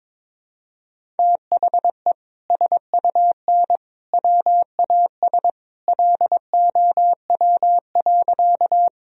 Morse code, heard as a keyed tone, about 22 wpm, 700 Hz. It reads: THE SUN WAS LOW.
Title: Sentences from Top 1000 Words - Review Part 1 - 22wpm